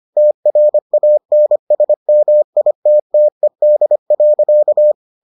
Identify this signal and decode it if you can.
Morse code, heard as a keyed tone, about 25 wpm, 600 Hz. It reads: TRANSMITTED.